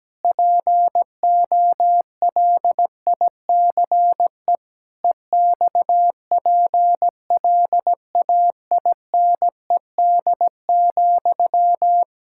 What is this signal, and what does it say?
Morse code, heard as a keyed tone, about 17 wpm, 700 Hz. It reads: POLICE EXPLAINED,
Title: Pangram Challenge v2 - 17wpm